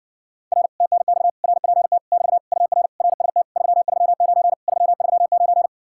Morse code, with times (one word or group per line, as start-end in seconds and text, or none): 0.5-0.7 R
0.8-1.3 TNX
1.4-2.0 RPT
2.1-2.4 <BT>
2.5-2.9 UR
3.0-3.4 RST
3.6-4.5 339
4.7-5.7 339